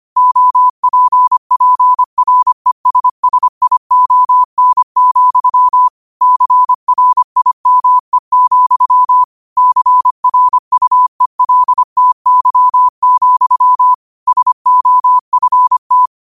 Morse code, heard as a keyed tone, about 25 wpm, 1000 Hz. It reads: OPPRESSION, CRIME, CRUELTY, SOFT